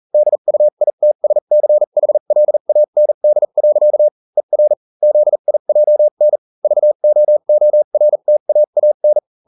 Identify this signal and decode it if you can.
Morse code, heard as a keyed tone, about 40 wpm, spaced as for 35 wpm, 600 Hz. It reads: DUITSCHLAND. ER ZIJN VOORTAAN